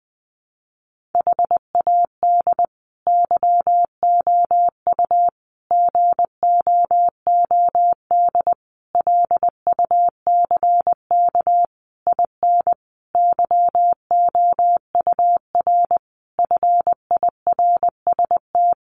Morse code, heard as a keyed tone, about 20 wpm, 700 Hz. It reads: HAD YOU GOOD LUCK IN YOUR FIRST
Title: Sentences from Top 1500 Words 20wpm